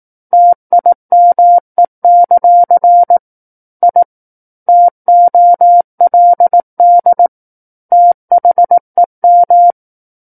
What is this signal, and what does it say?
Morse code, heard as a keyed tone, about 18 wpm, 700 Hz. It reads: TIME; I TOLD THEM
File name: Book One / 09-Chapter 9 - 18 WPM